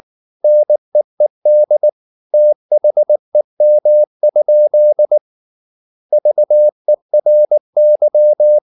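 Morse code, heard as a keyed tone, about 19 wpm, 600 Hz. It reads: NEED THEM? VERY
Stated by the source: Morse code at 19 wpm from CW med Gnister, reading Ken Follett's novel Never in English.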